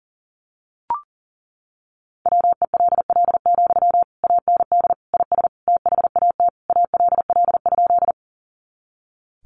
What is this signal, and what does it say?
Morse code, heard as a keyed tone, about 40 wpm, 700 Hz. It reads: WELL, AND IS THAT ALL?
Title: Sentences from Top 200 Words 40wpm